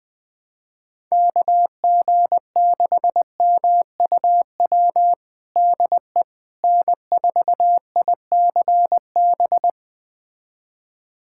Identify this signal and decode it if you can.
Morse code, heard as a keyed tone, about 20 wpm, 700 Hz. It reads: KG6MUW DE N4ICB
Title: USA Callsign DE Callsign 20wpm